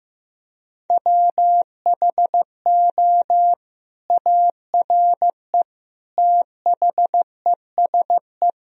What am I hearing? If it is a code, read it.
Morse code, heard as a keyed tone, about 16 wpm, 700 Hz. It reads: WHO ARE THESE
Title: Sentences from Top 1000 Words - Review Part 1 - 15wpm